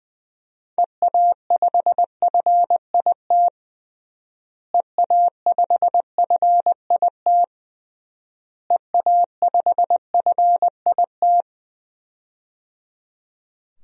Morse code, heard as a keyed tone, about 21 wpm, 700 Hz. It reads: EA5FIT EA5FIT EA5FIT
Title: DXCC-100 Call Signs - 6 Characters - Repeated 3x - 20wpm